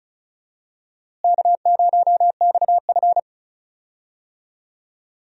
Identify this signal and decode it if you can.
Morse code, heard as a keyed tone, about 36 wpm, 700 Hz. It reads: K0XF
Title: USA Call Signs - 4 Characters - Repeated 2x - 35wpm